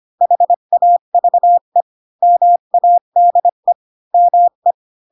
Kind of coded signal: Morse code